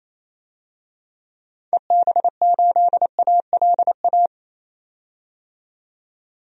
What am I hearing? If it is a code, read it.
Morse code, heard as a keyed tone, about 28 wpm, 700 Hz. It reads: EB8ALA
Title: DXCC-100 Call Signs - 6 Characters - 28wpm